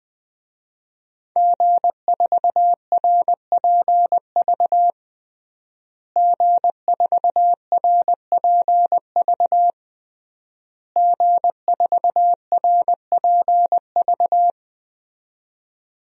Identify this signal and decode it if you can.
Morse code, heard as a keyed tone, about 21 wpm, 700 Hz. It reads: G4RPV G4RPV G4RPV
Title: DXCC-100 Call Signs - 5 Characters - Repeated 3x - 20wpm